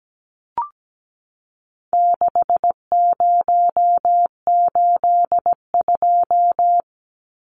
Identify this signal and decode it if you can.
Morse code, heard as a keyed tone, about 17 wpm, 700 Hz. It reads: 6082
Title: Numbers - 4-Digit 17wpm